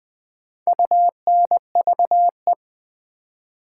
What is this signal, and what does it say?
Morse code, heard as a keyed tone, about 20 wpm, 700 Hz. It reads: UNVE
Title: Build A Word - 2 Letters At A Time - 6-Letter Words - 20wpm